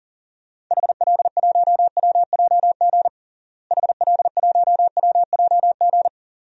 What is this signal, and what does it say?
Morse code, heard as a keyed tone, about 40 wpm, 700 Hz. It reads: HL1WJG HL1WJG
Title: DXCC-100 Call Signs - 6 Characters - Repeated 2x - 40wpm